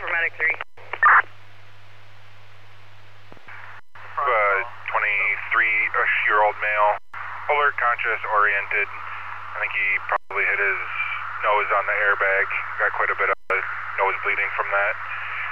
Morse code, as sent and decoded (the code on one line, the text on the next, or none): none